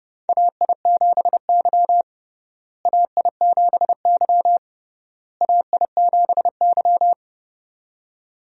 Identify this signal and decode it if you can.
Morse code, heard as a keyed tone, about 28 wpm, 700 Hz. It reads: AI7Y AI7Y AI7Y
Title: USA Call Signs - 4 Characters - Repeated 3x - 30wpm